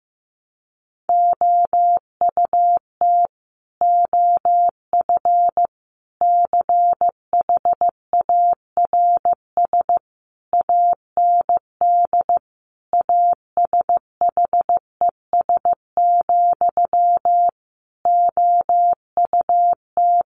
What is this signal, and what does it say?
Morse code, heard as a keyed tone, about 15 wpm, 700 Hz. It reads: OUT OF CHARS AND ASHES, OUT
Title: A Sound Of Thunder 15wpm